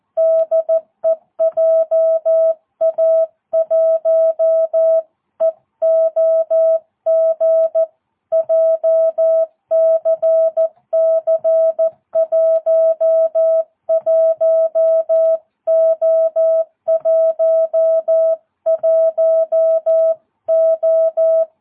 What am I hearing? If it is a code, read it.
Morse code, heard as a keyed tone, about 14 wpm, 660 Hz. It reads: DE JA1EOG JCC11O11O
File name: jh8cbh / 2022cw